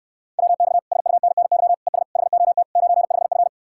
Morse code, heard as a keyed tone, about 69 wpm, 700 Hz. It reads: WX SUNNY ES HOT 95F